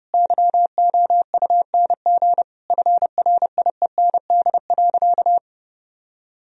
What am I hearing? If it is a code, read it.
Morse code, heard as a keyed tone, about 30 wpm, 700 Hz. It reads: YOUNG FRIEND.
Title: Sentences from Top 1000 Words - Review Part 3 - 30wpm